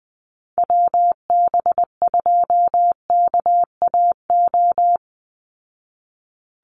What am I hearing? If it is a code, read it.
Morse code, heard as a keyed tone, about 20 wpm, 700 Hz. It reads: WB2KAO
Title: Call Signs Repeated 3x - 20wpm